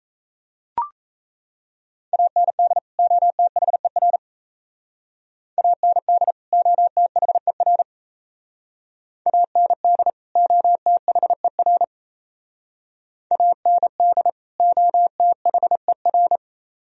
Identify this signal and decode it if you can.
Morse code, heard as a keyed tone, about 34 wpm, 700 Hz. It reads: AND OTHER AND OTHER AND OTHER AND OTHER
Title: Speed-Racing - Sets of 2 Words 28wpm